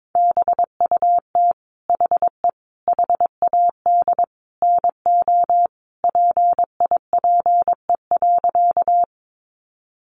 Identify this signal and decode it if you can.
Morse code, heard as a keyed tone, about 22 wpm, 700 Hz. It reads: BUT HE HAD NO PIPE.
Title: Sentences from Top 1500 Words 22wpm